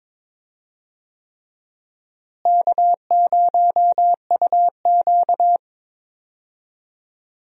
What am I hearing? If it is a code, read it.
Morse code, heard as a keyed tone, about 22 wpm, 700 Hz. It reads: K0UQ
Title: USA Call Signs - 4 Characters - Repeated 3x - 22wpm